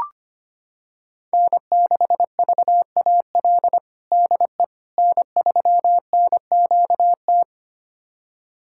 Morse code, heard as a keyed tone, about 25 wpm, 700 Hz. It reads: N6VAL DE N3NQT
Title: USA Callsign DE Callsign - 5 Characters - 25wpm